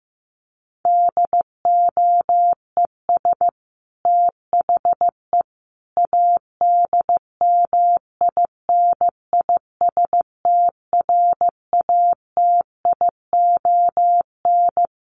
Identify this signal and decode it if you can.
Morse code, heard as a keyed tone, about 15 wpm, 700 Hz. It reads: DOES THE ADMINISTRATION
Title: Sentences from Top 1300 Words 15wpm